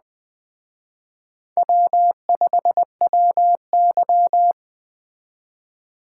Morse code, heard as a keyed tone, about 19 wpm, 700 Hz. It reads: W5WY